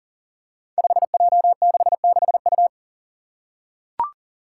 Morse code, heard as a keyed tone, about 38 wpm, 700 Hz. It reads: 5J66U
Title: Five Challenging Letter-Number Combinations 40wpm